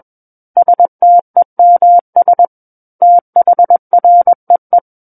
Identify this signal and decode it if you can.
Morse code, heard as a keyed tone, about 21 wpm, 700 Hz. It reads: STEMS THREE
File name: Book One / 12-Chapter 12 - 21 WPM